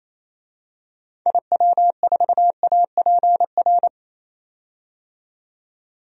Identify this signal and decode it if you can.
Morse code, heard as a keyed tone, about 28 wpm, 700 Hz. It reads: IW4APR
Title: DXCC-100 Call Signs - 6 Characters - 28wpm